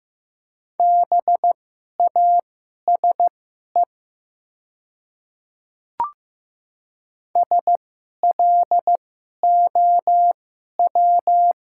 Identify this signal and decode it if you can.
Morse code, heard as a keyed tone, about 15 wpm, spaced as for 11 wpm, 700 Hz. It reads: BASE SLOW
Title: 2x Character Spacing - Top 500 Words - 15wpm